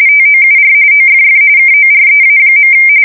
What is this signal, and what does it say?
RTTY, about 45.45 baud. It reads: TOUR OFF KOREA.
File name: fsk170 / battleshipnj